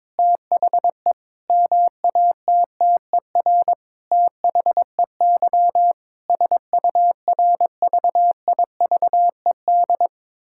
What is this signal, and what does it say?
Morse code, heard as a keyed tone, about 22 wpm, 700 Hz. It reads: THE MATTER THEY SURVIVED
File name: Sentences with 7-Word Phrases - 22wpm